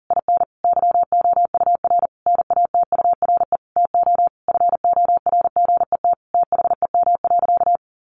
Morse code, heard as a keyed tone, about 40 wpm, 700 Hz. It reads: IN YOUR NATURE TO FORGET THEM.
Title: Sentences from Top 1200 Words 40wpm